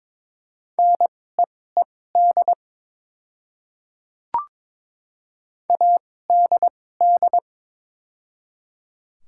Morse code, heard as a keyed tone, about 22 wpm, spaced as for 16 wpm, 700 Hz. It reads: NEED ADD
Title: 2x Character Spacing - Top 200 Words - 22wpm